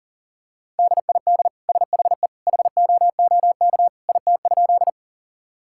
Morse code, heard as a keyed tone, about 40 wpm, 700 Hz. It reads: DID SHE HOOK IT?
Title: Sentences from Top 1600 Words 40wpm